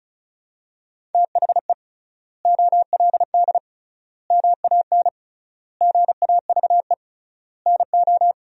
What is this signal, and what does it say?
Morse code, heard as a keyed tone, about 36 wpm, 700 Hz. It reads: THE OLD MAN GAVE NO
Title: 3x Word-Spacing - Sentences from Top 500 Words - 35wpm